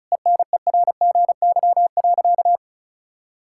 Morse code, heard as a keyed tone, about 34 wpm, 700 Hz. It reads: ENERGY.
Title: Sentences from Top 1000 Words - Review Part 6 - 35wpm